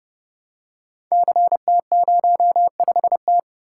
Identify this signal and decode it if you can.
Morse code, heard as a keyed tone, about 30 wpm, 700 Hz. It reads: CT05T
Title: Code Groups - 5 Letter:Numbers - 30wpm